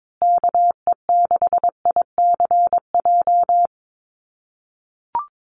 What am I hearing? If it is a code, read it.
Morse code, heard as a keyed tone, about 22 wpm, 700 Hz. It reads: KE6ICJ
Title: USA Call Signs Repeated 2x 22wpm